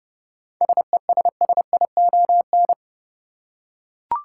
Morse code, heard as a keyed tone, about 31 wpm, 700 Hz. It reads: SESSION
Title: Seven Letter Words - 30wpm